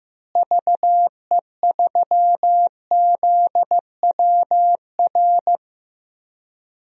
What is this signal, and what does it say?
Morse code, heard as a keyed tone, about 15 wpm, 700 Hz. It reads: VE3ZWR
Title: DXCC-100 Call Signs - 6 Characters - Repeated 3x - 15wpm